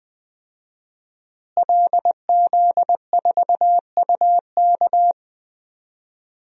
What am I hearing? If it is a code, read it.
Morse code, heard as a keyed tone, about 20 wpm, 700 Hz. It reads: LZ4UK
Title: DXCC-100 Call Signs - 5 Characters - Repeated 3x - 20wpm